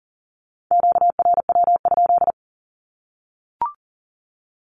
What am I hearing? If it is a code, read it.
Morse code, heard as a keyed tone, about 40 wpm, 700 Hz. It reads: QRW?
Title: Q-Codes Repeated 3x 40wpm